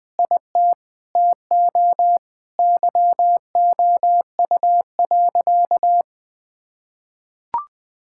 Morse code, heard as a keyed tone, about 21 wpm, 700 Hz. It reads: IT TO YOU.
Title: Sentences with 5-Word Phrases - 20wpm